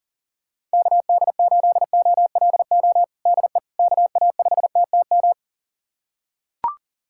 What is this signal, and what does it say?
Morse code, heard as a keyed tone, about 37 wpm, 700 Hz. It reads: KD8OLO DE KA5TTM
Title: USA Callsign DE Callsign 40wpm